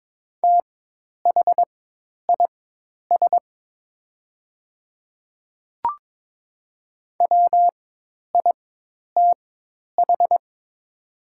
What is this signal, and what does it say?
Morse code, heard as a keyed tone, about 22 wpm, spaced as for 10 wpm, 700 Hz. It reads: THIS WITH
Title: 4x Character Spacing - Top 100 Words - 22wpm